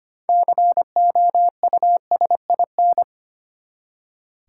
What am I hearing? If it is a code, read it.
Morse code, heard as a keyed tone, about 25 wpm, 700 Hz. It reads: COUSIN
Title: Six Letter Words - 25wpm